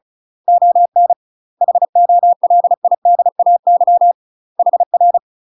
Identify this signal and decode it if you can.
Morse code, heard as a keyed tone, about 36 wpm, 700 Hz. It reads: ON HOLIDAY HR
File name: UK MASTER QSOs TRAINING / 35WPM / UK MASTER QSOs 0043